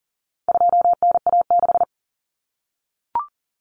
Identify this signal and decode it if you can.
Morse code, heard as a keyed tone, about 40 wpm, 700 Hz. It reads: 2NA6
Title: Common Cut Numbers - 4-Digit 40wpm